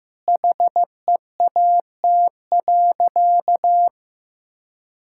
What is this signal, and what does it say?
Morse code, heard as a keyed tone, about 16 wpm, 700 Hz. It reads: HEAT.
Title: Sentences from Top 1000 Words - Review Part 6 - 15wpm